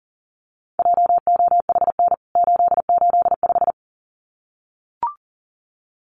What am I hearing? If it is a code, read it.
Morse code, heard as a keyed tone, about 40 wpm, 700 Hz. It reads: JOHN 885
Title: CWT Exchange 40wpm